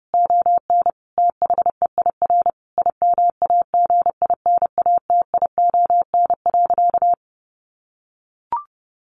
Morse code, heard as a keyed tone, about 30 wpm, 700 Hz. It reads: ON THEIR IMAGINATION.
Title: Sentences from Top 1300 Words 30wpm